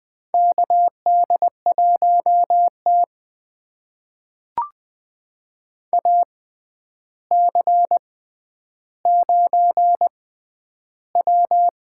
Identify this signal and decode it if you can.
Morse code, heard as a keyed tone, about 20 wpm, spaced as for 10 wpm, 700 Hz. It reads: KD1T AC9W
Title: USA Call Signs - 4 Characters - 6x Character Spacing - 20wpm